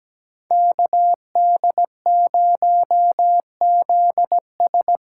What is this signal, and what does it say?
Morse code, heard as a keyed tone, about 20 wpm, 700 Hz. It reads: KD0ZS
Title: Speed Racing - USA 2x2 Call Signs 17wpm